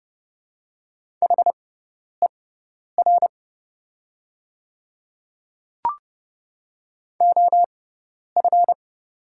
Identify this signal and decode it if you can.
Morse code, heard as a keyed tone, about 30 wpm, spaced as for 10 wpm, 700 Hz. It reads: HER OF
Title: 6x Character Spacing - Top 100 Words - 30wpm